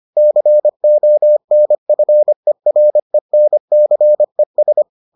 Morse code, heard as a keyed tone, about 25 wpm, 600 Hz. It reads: CONFERENCES